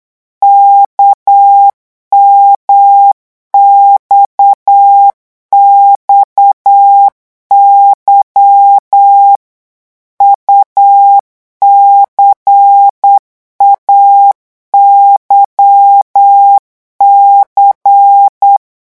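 Morse code, heard as a keyed tone, about 8 wpm, 800 Hz. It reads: KMXXY UCAYC
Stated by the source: Morse code revision lesson at 8 words per minute, from a French amateur radio course.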